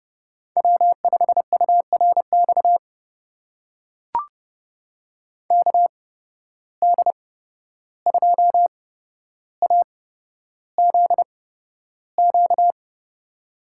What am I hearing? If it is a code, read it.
Morse code, heard as a keyed tone, about 30 wpm, spaced as for 12 wpm, 700 Hz. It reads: W5URX KD2AZQ
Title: USA Call Signs - 8x Character Spacing - 30wpm